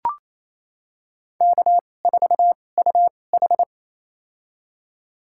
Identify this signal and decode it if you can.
Morse code, heard as a keyed tone, about 28 wpm, spaced as for 20 wpm, 700 Hz. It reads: K4UH
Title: USA Call Signs - 4 Characters - 2x Character Spacing - 28wpm